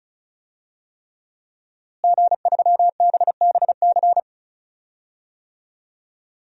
Morse code, heard as a keyed tone, about 36 wpm, 700 Hz. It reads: G3BBC
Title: DXCC-100 Call Signs - 5 Characters - Repeated 3x - 35wpm